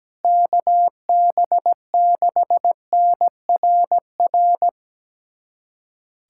Morse code, heard as a keyed tone, about 17 wpm, 700 Hz. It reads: KB6NRR
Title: USA Callsign DE Callsign 17wpm